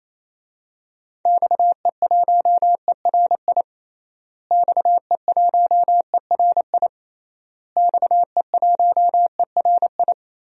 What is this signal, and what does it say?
Morse code, heard as a keyed tone, about 27 wpm, 700 Hz. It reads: XE1ERI XE1ERI XE1ERI
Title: DXCC-100 Call Signs - 6 Characters - Repeated 3x - 28wpm